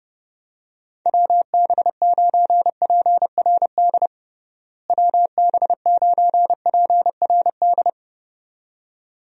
Morse code, heard as a keyed tone, about 28 wpm, 700 Hz. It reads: WB9PRD WB9PRD
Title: USA Call Signs - 6 Characters - Repeated 2x - 30wpm